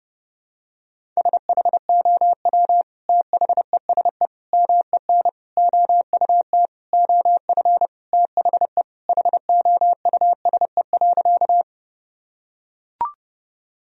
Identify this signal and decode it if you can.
Morse code, heard as a keyed tone, about 30 wpm, 700 Hz. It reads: SHOW THESE MEN OUT OF THE HOUSE.
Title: Sentences from Top 1000 Words - Review Part 1 - 30wpm